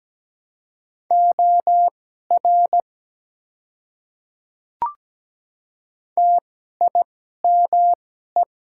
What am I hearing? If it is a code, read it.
Morse code, heard as a keyed tone, about 17 wpm, spaced as for 12 wpm, 700 Hz. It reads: OR TIME
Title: 2x Character Spacing - Top 100 Words - 17wpm